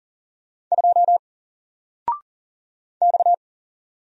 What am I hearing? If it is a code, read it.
Morse code, heard as a keyed tone, about 40 wpm, 700 Hz. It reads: 2 X
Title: Single Character (Letter:Number:Punctuation:Prosign) Alt 40wpm